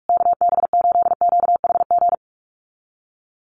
Morse code, heard as a keyed tone, about 44 wpm, 700 Hz. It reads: KB8QHG
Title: USA Call Signs Repeated 2x 45wpm